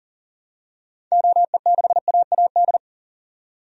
Morse code, heard as a keyed tone, about 40 wpm, 700 Hz. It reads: OE6AAD
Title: DXCC-100 Call Signs - 6 Characters - 40wpm